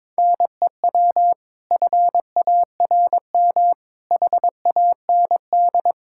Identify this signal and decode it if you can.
Morse code, heard as a keyed tone, about 22 wpm, 700 Hz. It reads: NEW FARM HAND
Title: Pangram Challenge v2 - 22wpm